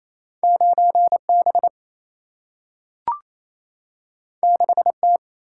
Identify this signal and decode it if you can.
Morse code, heard as a keyed tone, about 28 wpm, 700 Hz. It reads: 9B 6T
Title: Single-Letter Single-Number 28wpm